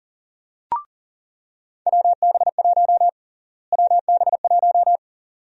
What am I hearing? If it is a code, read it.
Morse code, heard as a keyed tone, about 40 wpm, 700 Hz. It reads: WB1 WB1
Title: USA Call Sign Prefixes - Repeated 2x - 40wpm